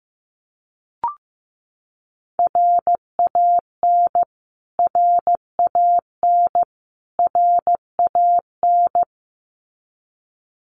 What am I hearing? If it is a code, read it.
Morse code, heard as a keyed tone, about 15 wpm, 700 Hz. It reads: RAN RAN RAN
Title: Top 500 Words Repeated 15wpm